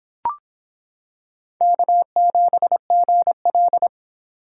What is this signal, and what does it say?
Morse code, heard as a keyed tone, about 26 wpm, 700 Hz. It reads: K7GL